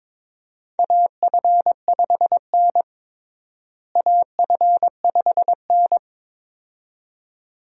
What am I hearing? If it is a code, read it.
Morse code, heard as a keyed tone, about 22 wpm, 700 Hz. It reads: AF5N AF5N
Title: USA Call Signs - 4 Characters - Repeated 2x - 22wpm